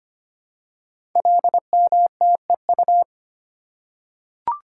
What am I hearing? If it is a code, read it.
Morse code, heard as a keyed tone, about 25 wpm, 700 Hz. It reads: LMTEU
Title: Code Groups - 5 Letters - 25wpm